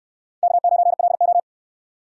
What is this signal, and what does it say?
Morse code, heard as a keyed tone, about 69 wpm, 700 Hz. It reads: K9LO